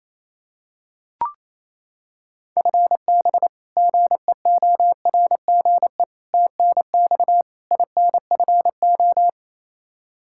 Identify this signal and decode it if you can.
Morse code, heard as a keyed tone, about 27 wpm, 700 Hz. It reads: FB GEORGE TNX INFO
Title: K1USN SST QSO Practice 28wpm